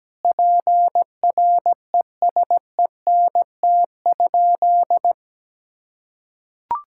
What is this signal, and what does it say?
Morse code, heard as a keyed tone, about 17 wpm, 700 Hz. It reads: PRESENT?
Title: Sentences from Top 1000 Words - Review Part 6 - 17wpm